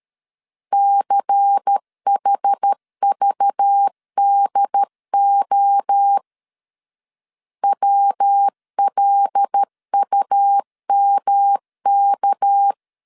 Morse code, heard as a keyed tone, about 13 wpm, 800 Hz. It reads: CHVDO WLUMK
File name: LEZIONI / 6 PKX0 / 6 PK 12wpm 2006